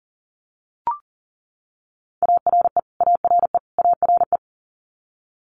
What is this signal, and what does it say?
Morse code, heard as a keyed tone, about 39 wpm, 700 Hz. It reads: ARE ARE ARE
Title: Multi-letter Phonemes Repeated 3x 40wpm